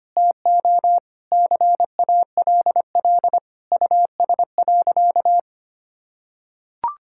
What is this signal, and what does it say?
Morse code, heard as a keyed tone, about 25 wpm, 700 Hz. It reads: TO CALL US.
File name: practice / 25wpm / Sentences with 4-Word Phrases - 25wpm